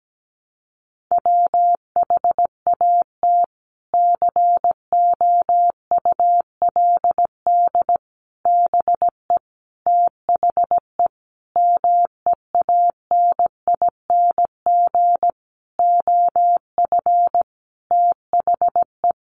Morse code, heard as a keyed tone, about 17 wpm, 700 Hz. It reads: WHAT COULD BE THE MEANING OF THE
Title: Sentences from Top 900 Words 17wpm